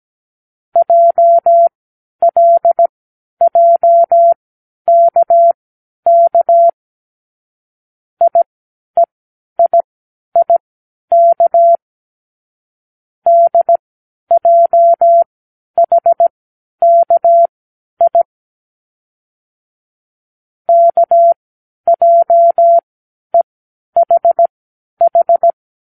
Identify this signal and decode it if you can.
Morse code, heard as a keyed tone, about 17 wpm, spaced as for 10 wpm, 680 Hz. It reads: JLJKK IEIIK DJHKI KJEHH
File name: k7qo / Lesson18(A-L)